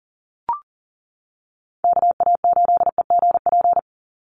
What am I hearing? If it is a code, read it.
Morse code, heard as a keyed tone, about 40 wpm, 700 Hz. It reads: KA8EGP
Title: USA Call Signs 40wpm